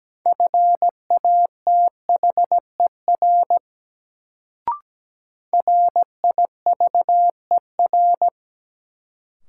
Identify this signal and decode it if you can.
Morse code, heard as a keyed tone, about 17 wpm, 700 Hz. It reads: FATHER RIVER
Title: Sentences from Top 300 Words - New Word Drill - 17wpm